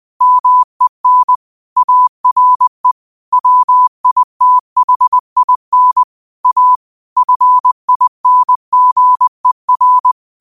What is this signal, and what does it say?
Morse code, heard as a keyed tone, about 20 wpm, 1000 Hz. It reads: MEN ARE WITHIN A FINGER